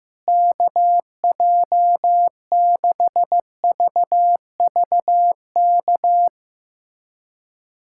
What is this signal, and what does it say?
Morse code, heard as a keyed tone, about 15 wpm, 700 Hz. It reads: KJ6VVK
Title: USA Callsign DE Callsign - 6 Characters - 15wpm